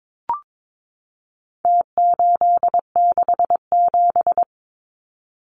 Morse code, heard as a keyed tone, about 22 wpm, 700 Hz. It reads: T867
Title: Numbers - Contest Style - 4-Digit 22wpm